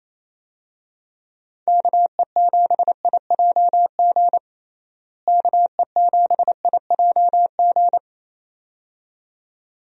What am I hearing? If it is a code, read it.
Morse code, heard as a keyed tone, about 28 wpm, 700 Hz. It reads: KE7IJG KE7IJG
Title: USA Call Signs - 6 Characters - Repeated 2x - 28wpm